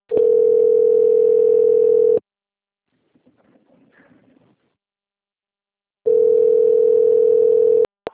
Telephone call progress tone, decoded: ringback tone

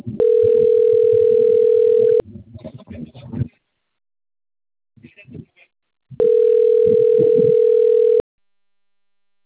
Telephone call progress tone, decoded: ringback tone

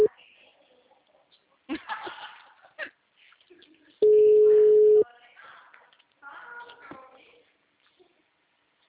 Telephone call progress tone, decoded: ringback tone

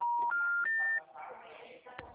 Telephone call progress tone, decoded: special information tone

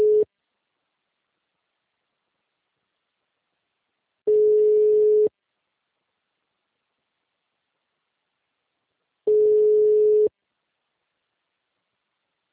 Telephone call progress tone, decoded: ringback tone

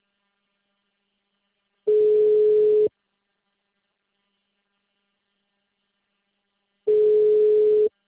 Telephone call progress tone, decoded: ringback tone

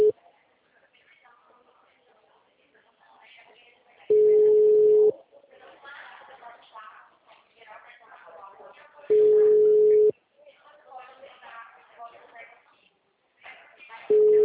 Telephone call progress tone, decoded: ringback tone

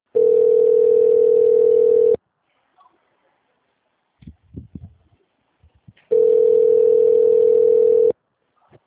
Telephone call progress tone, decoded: ringback tone

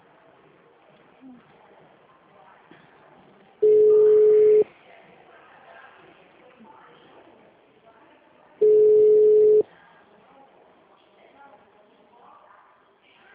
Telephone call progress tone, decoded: ringback tone

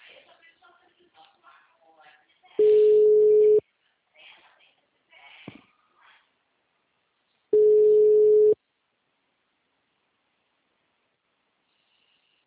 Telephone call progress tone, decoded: ringback tone

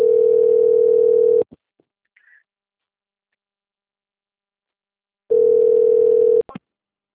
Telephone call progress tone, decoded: ringback tone

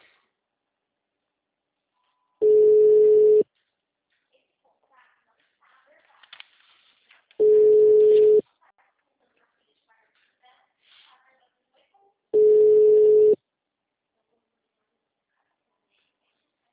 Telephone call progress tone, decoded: ringback tone

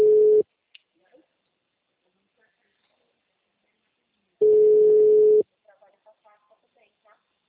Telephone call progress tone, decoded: ringback tone

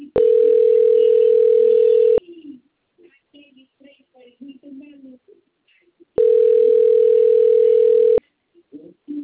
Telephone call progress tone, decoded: ringback tone

